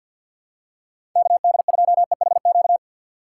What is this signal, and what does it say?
Morse code, heard as a keyed tone, about 52 wpm, 700 Hz. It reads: KD2EHQ
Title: USA Call Signs - 2x Character Spacing - 50wpm